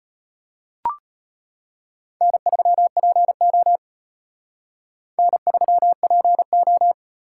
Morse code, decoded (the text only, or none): N3PO N3PO